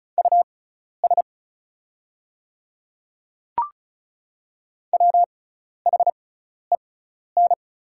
Morse code, heard as a keyed tone, about 36 wpm, spaced as for 12 wpm, 690 Hz. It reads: US WHEN
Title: 6x Character Spacing - Top 200 Words - 35wpm